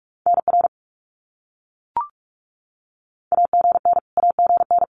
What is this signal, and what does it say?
Morse code, heard as a keyed tone, about 44 wpm, 700 Hz. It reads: NR AGN AGN
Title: Abbreviations Repeated 3x - 45wpm